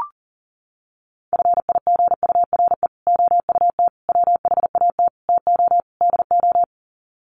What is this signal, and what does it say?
Morse code, heard as a keyed tone, about 37 wpm, 700 Hz. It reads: FIGURE OUT WHAT TO DO
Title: Sets of 5 Words Encore2 40wpm